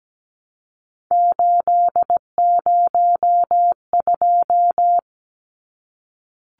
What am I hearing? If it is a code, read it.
Morse code, heard as a keyed tone, about 17 wpm, 700 Hz. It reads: 802
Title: Numbers - 3-Digit 17wpm